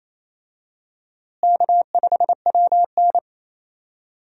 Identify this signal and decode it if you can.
Morse code, heard as a keyed tone, about 27 wpm, 700 Hz. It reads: K5WN
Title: USA Call Signs - 4 Characters - Repeated 3x - 28wpm